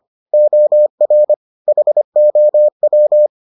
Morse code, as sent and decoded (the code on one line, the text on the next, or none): --- .-. / .... --- .--
OR HOW